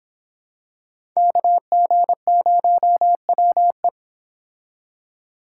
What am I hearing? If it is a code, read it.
Morse code, heard as a keyed tone, about 26 wpm, 700 Hz. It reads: KG0WE